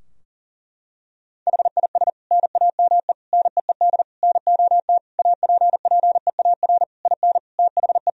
Morse code, heard as a keyed tone, about 37 wpm, 700 Hz. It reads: HIS NAME NEED NOT APPEAR IN THE
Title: Sentences from Top 1000 Words - Review Part 2 - 40wpm